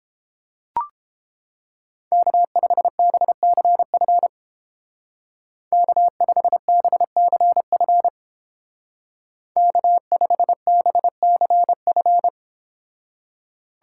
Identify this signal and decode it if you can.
Morse code, heard as a keyed tone, about 30 wpm, 700 Hz. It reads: K5BCF K5BCF K5BCF